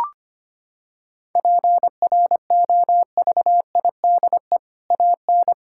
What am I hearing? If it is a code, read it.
Morse code, heard as a keyed tone, about 25 wpm, 700 Hz. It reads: PROVIDE AN
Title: Sentences from Top 1000 Words - Review Part 5 - 25wpm